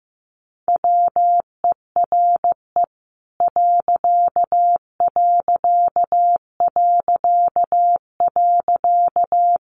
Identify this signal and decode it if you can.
Morse code, heard as a keyed tone, about 14 wpm, 700 Hz. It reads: WERE ....
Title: A Sound Of Thunder 15wpm